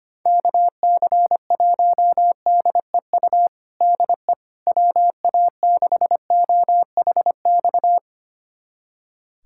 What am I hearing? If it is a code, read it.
Morse code, heard as a keyed tone, about 25 wpm, 700 Hz. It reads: KC1DEU DE WA6OHX
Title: USA Callsign DE Callsign - 6 Characters - 25wpm